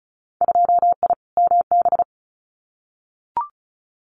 Morse code, heard as a keyed tone, about 36 wpm, 700 Hz. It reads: 2I MB
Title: Winter Field Day Exchanges Encore 35wpm